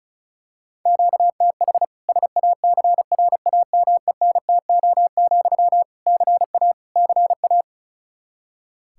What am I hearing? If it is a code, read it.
Morse code, heard as a keyed tone, about 36 wpm, 700 Hz. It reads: QTH SACRAMENTO, CA CA
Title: Practice QSO - Exchange 1 - Beginning - 35wpm